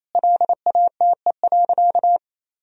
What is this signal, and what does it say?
Morse code, heard as a keyed tone, about 28 wpm, 700 Hz. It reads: LATE.
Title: Sentences from Top 1000 Words - Review Part 3 - 28wpm